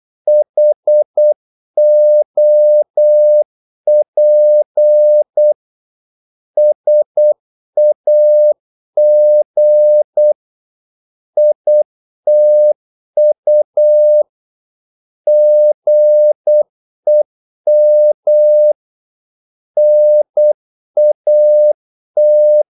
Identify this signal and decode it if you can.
Morse code, heard as a keyed tone, about 8 wpm, 600 Hz. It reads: HOP SAG ITU GEM NAT
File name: Korte ord DK 8wpm